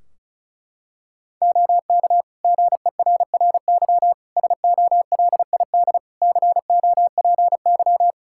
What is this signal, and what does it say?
Morse code, heard as a keyed tone, about 34 wpm, 700 Hz. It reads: OK GERRY SOLID COPY
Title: Practice QSO - Exchange 3 - More Info - 35wpm